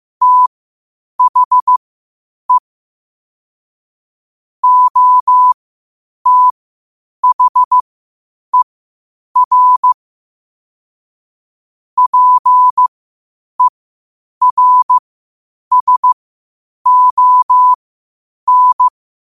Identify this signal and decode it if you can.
Morse code, heard as a keyed tone, about 15 wpm, spaced as for 8 wpm, 1000 Hz. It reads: THE OTHER PERSON
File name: qotdfiles / 8wpm